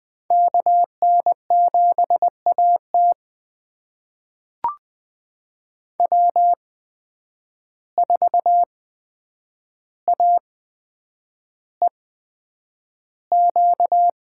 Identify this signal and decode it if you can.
Morse code, heard as a keyed tone, about 20 wpm, spaced as for 9 wpm, 700 Hz. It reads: KN7AT W4AEQ
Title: USA Call Signs - 5 Characters - 8x Character Spacing - 20wpm